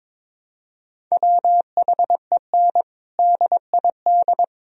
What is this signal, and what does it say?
Morse code, heard as a keyed tone, about 22 wpm, 700 Hz. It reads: WHEN DID
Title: Sentences from Top 1000 Words - Review Part 4 - 22wpm